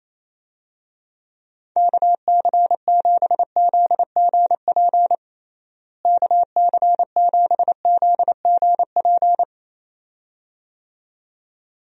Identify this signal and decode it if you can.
Morse code, heard as a keyed tone, about 28 wpm, 700 Hz. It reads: KC7ZGP KC7ZGP
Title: USA Call Signs - 6 Characters - Repeated 2x - 28wpm